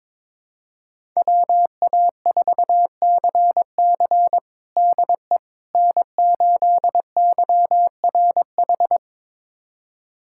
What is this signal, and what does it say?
Morse code, heard as a keyed tone, about 22 wpm, 700 Hz. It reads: WA4CC DE N8YRH
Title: USA Callsign DE Callsign - 5 Characters - 22wpm